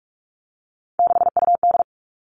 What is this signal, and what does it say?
Morse code, heard as a keyed tone, about 46 wpm, 700 Hz. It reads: 6UD
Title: Three Challenging Letter-Number Combinations 45wpm